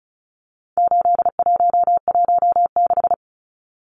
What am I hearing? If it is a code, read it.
Morse code, heard as a keyed tone, about 34 wpm, 700 Hz. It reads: 8116
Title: Numbers - 4-Digit 35wpm